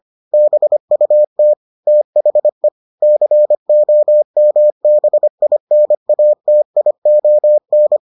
Morse code, heard as a keyed tone, about 25 wpm, 600 Hz. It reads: BUT THE COMBINATION